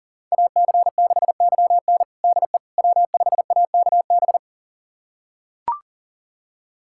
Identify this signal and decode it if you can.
Morse code, heard as a keyed tone, about 40 wpm, 700 Hz. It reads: AC6YN DE W5AKB
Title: USA Callsign DE Callsign - 5 Characters - 40wpm